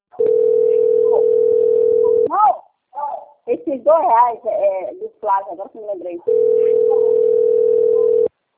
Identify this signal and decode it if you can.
Ringback tone (North America), 440 + 480 Hz, with speech.
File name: out- - - - - .7549